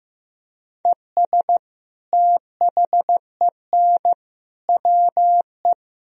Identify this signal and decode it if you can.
Morse code, heard as a keyed tone, about 16 wpm, 700 Hz. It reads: ES THEN WE
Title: Sentences from Top 300 Words and Abbreviations 15wpm